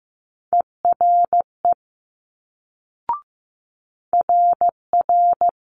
Morse code, heard as a keyed tone, about 15 wpm, 700 Hz. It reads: ERE RR
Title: Multi-letter Phonemes Quick Fire 15wpm